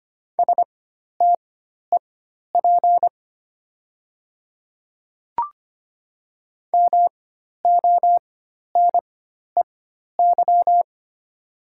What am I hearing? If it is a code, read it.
Morse code, heard as a keyed tone, about 25 wpm, spaced as for 12 wpm, 700 Hz. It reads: STEP MONEY
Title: 4x Character Spacing - Top 500 Words - 25wpm